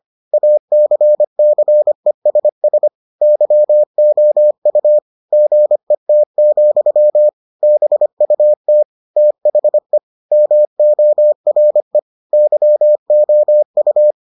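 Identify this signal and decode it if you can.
Morse code, heard as a keyed tone, about 25 wpm, 600 Hz. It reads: ACCESS YOU GET, BUT THE MORE YOU
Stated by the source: Morse edition of the Foundations of Amateur Radio episodes.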